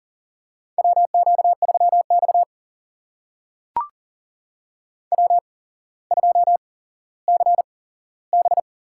Morse code, heard as a keyed tone, about 40 wpm, spaced as for 17 wpm, 700 Hz. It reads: WQ3X W2CB
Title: USA Call Signs - 4 Characters - 8x Character Spacing - 40wpm